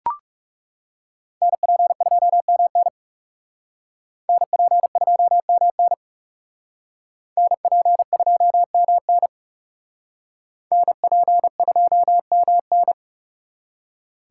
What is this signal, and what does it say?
Morse code, heard as a keyed tone, about 37 wpm, 700 Hz. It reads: NP2MN NP2MN NP2MN NP2MN
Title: Speed Racing - USA Call Signs 30wpm